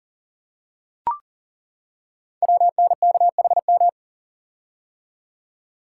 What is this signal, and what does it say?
Morse code, heard as a keyed tone, about 39 wpm, 690 Hz. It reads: WNKHM
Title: Code Groups - 5 Letters - 40wpm